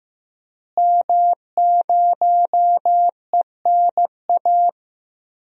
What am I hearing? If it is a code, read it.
Morse code, heard as a keyed tone, about 15 wpm, 700 Hz. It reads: M0ENA
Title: DXCC-100 Call Signs - 5 Characters - Repeated 3x - 15wpm